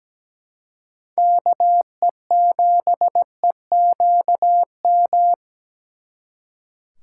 Morse code, heard as a keyed tone, about 20 wpm, 700 Hz. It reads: KE7EQM
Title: Speed Racing - USA 2x3 Call Signs 17wpm